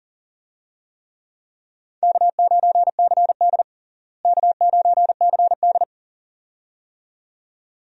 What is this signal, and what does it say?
Morse code, heard as a keyed tone, about 40 wpm, 700 Hz. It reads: K9CD K9CD
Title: USA Call Signs - 4 Characters - Repeated 2x - 40wpm